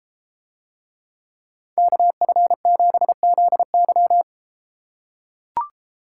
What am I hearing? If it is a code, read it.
Morse code, heard as a keyed tone, about 31 wpm, 700 Hz. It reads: KF7ZY